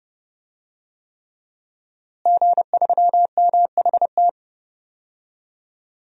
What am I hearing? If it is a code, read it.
Morse code, heard as a keyed tone, about 30 wpm, 700 Hz. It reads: G3MHT
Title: DXCC-100 Call Signs - 5 Characters - Repeated 3x - 30wpm